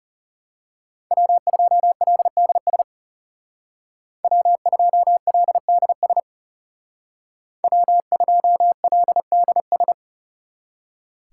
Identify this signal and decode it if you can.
Morse code, heard as a keyed tone, about 36 wpm, 700 Hz. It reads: W2LDS W2LDS W2LDS
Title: Speed Racing - USA 1x3 Call Signs 30wpm